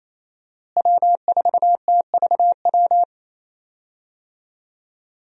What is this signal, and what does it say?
Morse code, heard as a keyed tone, about 28 wpm, 700 Hz. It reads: W4TVW
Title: USA Call Signs - 5 Characters - Repeated 3x - 28wpm